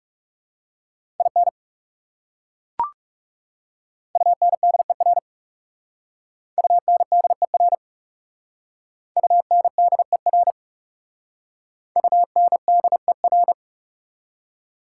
Morse code, heard as a keyed tone, about 37 wpm, 690 Hz. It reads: IN UNDER UNDER UNDER UNDER
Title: Speed-Racing Prefixes and Suffixes 30wpm